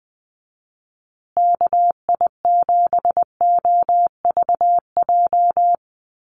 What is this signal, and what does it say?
Morse code, heard as a keyed tone, about 20 wpm, 700 Hz. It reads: KI7OVJ
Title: USA Call Signs Repeated 3x 20wpm